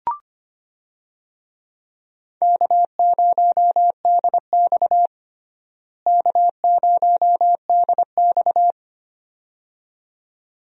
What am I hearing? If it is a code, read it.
Morse code, heard as a keyed tone, about 25 wpm, 700 Hz. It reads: K0DX K0DX
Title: USA Call Signs - 4 Characters - Repeated 2x - 25wpm